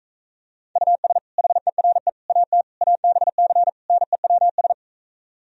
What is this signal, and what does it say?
Morse code, heard as a keyed tone, about 45 wpm, 700 Hz. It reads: US HERE AT ABC NEWS